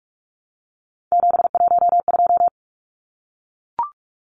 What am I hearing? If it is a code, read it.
Morse code, heard as a keyed tone, about 44 wpm, 700 Hz. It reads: 712
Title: Numbers - 3-Digit 45wpm